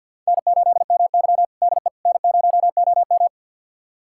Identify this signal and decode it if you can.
Morse code, heard as a keyed tone, about 48 wpm, 700 Hz. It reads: N8MY DE N0OM
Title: USA Callsign DE Callsign - 4 Characters - 50wpm